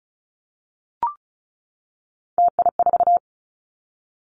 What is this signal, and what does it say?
Morse code, heard as a keyed tone, about 36 wpm, 700 Hz. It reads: TI4
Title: Callsign Prefixes for Top 100 DXCC - 35wpm